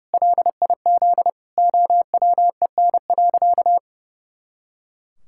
Morse code, heard as a keyed tone, about 30 wpm, 700 Hz. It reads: LIZ OWEN.
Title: Pangram Challenge v2 - 30wpm